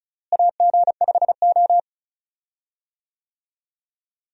Morse code, heard as a keyed tone, about 36 wpm, 700 Hz. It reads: AG5O
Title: USA Call Signs - 4 Characters - Repeated 2x - 35wpm